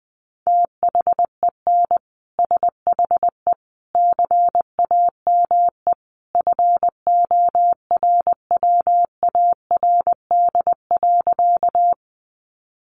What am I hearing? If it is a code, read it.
Morse code, heard as a keyed tone, about 20 wpm, 700 Hz. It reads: THEN SHE CAME FORWARD.
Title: Sentences from Top 1300 Words 20wpm